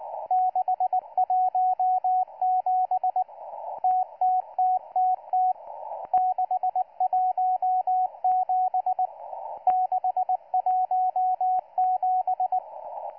Morse code, heard as a keyed tone, about 19 wpm, 740 Hz, with radio noise. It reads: 617 TTTTT 617 617